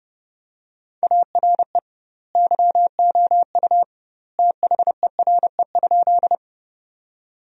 Morse code, heard as a keyed tone, about 30 wpm, 700 Hz. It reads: ARE YOU THERE?
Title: 2x Word-Spacing - Sentences from Top 100 Words - 30wpm